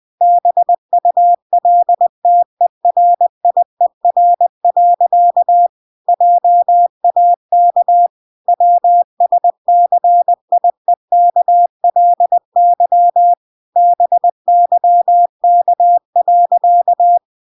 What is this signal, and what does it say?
Morse code, heard as a keyed tone, about 20 wpm, 700 Hz. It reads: BULTERIER. JAK WSCIEKLY BYK.